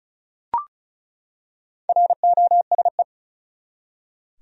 Morse code, heard as a keyed tone, about 34 wpm, 700 Hz. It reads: ROSE
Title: USA Common Names - 35wpm